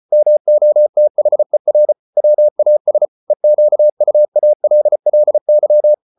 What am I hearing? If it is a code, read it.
Morse code, heard as a keyed tone, about 34 wpm, 600 Hz. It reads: MOTHER WAS EQUALLY